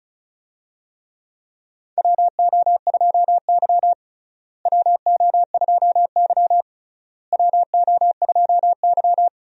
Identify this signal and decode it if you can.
Morse code, heard as a keyed tone, about 34 wpm, 700 Hz. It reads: WO2Y WO2Y WO2Y
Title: USA Call Signs - 4 Characters - Repeated 3x - 35wpm